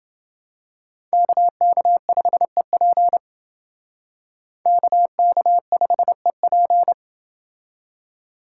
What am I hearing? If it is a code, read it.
Morse code, heard as a keyed tone, about 28 wpm, 700 Hz. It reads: KK5EP KK5EP